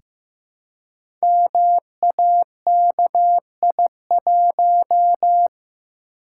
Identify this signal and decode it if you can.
Morse code, heard as a keyed tone, about 15 wpm, 700 Hz. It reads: MAKI1
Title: Code Groups - 5 Letter:Numbers - 15wpm